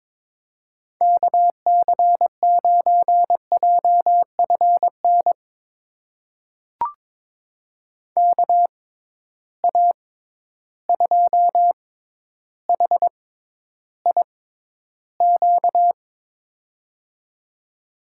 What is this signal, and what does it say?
Morse code, heard as a keyed tone, about 22 wpm, spaced as for 11 wpm, 700 Hz. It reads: KC9JFN KA2HIQ
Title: USA Call Signs - 6 Characters - 6x Character Spacing - 22wpm